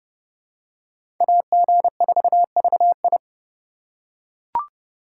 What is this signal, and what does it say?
Morse code, heard as a keyed tone, about 30 wpm, 700 Hz. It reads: AG4VI
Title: USA Call Signs - 5 Characters - 3x Character Spacing - 30wpm